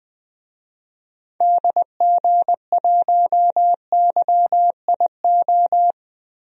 Morse code, heard as a keyed tone, about 20 wpm, 700 Hz. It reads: DG1YIO
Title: DXCC-100 Call Signs - 6 Characters - Repeated 2x - 20wpm